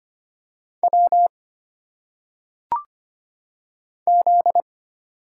Morse code, heard as a keyed tone, about 25 wpm, 700 Hz. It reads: W Z
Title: Single Letters - Phonetic Pronunciation 25wpm